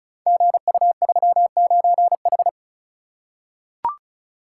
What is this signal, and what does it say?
Morse code, heard as a keyed tone, about 33 wpm, 700 Hz. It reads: GU39H
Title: Code Groups - 5 Letter:Numbers - 35wpm